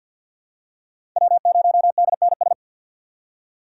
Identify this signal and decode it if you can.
Morse code, heard as a keyed tone, about 48 wpm, 700 Hz. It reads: W0DNS
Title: USA Call Signs - 5 Characters - 4x Character Spacing - 50wpm